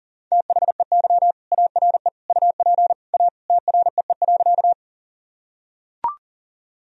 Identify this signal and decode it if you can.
Morse code, heard as a keyed tone, about 40 wpm, 700 Hz. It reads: THEY ARE UP A TREE.
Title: Sentences from Top 1000 Words - Review Part 1 - 40wpm